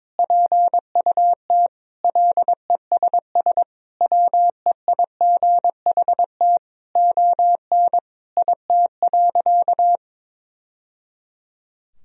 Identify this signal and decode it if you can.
Morse code, heard as a keyed tone, about 22 wpm, 700 Hz. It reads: PUT LESS WEIGHT ON IT.
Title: Sentences from Top 1000 Words - Review Part 3 - 22wpm